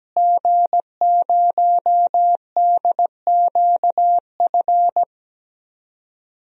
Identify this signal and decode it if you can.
Morse code, heard as a keyed tone, about 17 wpm, 700 Hz. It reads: G0DQF
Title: DXCC-100 Call Signs - 5 Characters - Repeated 3x - 17wpm